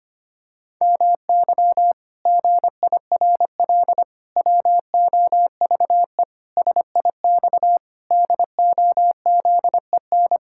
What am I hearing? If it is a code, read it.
Morse code, heard as a keyed tone, about 25 wpm, 700 Hz. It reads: MY GIRL WOVE SIX DOZEN
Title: Pangram Challenge v2 - 25wpm